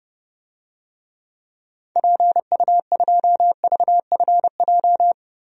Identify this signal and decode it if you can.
Morse code, heard as a keyed tone, about 30 wpm, 700 Hz. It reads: PU2VFJ
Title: DXCC-100 Call Signs - 6 Characters - Repeated 2x - 30wpm